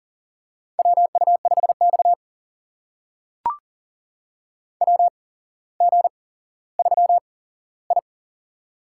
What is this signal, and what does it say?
Morse code, heard as a keyed tone, about 38 wpm, spaced as for 17 wpm, 700 Hz. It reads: WU5X WG3I
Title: USA Call Signs - 4 Characters - 8x Character Spacing - 40wpm